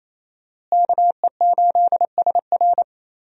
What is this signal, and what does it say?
Morse code, heard as a keyed tone, about 28 wpm, 700 Hz. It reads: KE8SR
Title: USA Call Signs - 5 Characters - Repeated 3x - 28wpm